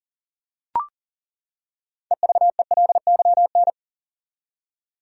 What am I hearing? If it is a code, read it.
Morse code, heard as a keyed tone, about 37 wpm, 700 Hz. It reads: EVELYN
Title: USA Common Names - 40wpm